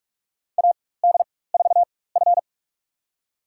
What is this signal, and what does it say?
Morse code, heard as a keyed tone, about 46 wpm, spaced as for 21 wpm, 700 Hz. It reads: AD4F
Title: USA Call Signs - 4 Characters - 4x Character Spacing - 45wpm